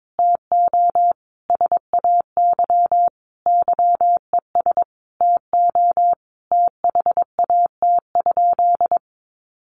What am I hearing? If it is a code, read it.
Morse code, heard as a keyed tone, about 22 wpm, 700 Hz. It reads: TO SAY YES TO THAT?
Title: Sentences from Top 1100 Words 22wpm